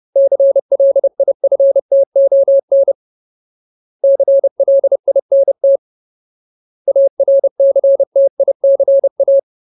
Morse code, heard as a keyed tone, about 30 wpm, 550 Hz. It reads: CLIFTON CLINT ARCTICA
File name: Ultimate Word List / 30WPM Ultimate